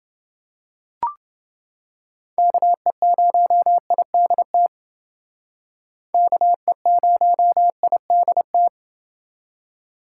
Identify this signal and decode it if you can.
Morse code, heard as a keyed tone, about 30 wpm, 700 Hz. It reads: KE0IDT KE0IDT